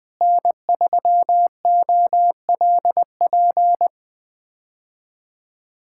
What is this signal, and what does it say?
Morse code, heard as a keyed tone, about 21 wpm, 700 Hz. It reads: N3OLP
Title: K1USN SST QSO Practice - 20wpm